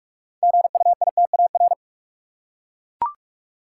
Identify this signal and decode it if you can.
Morse code, heard as a keyed tone, about 42 wpm, 700 Hz. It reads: GUITAR
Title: Six Letter Words - 45wpm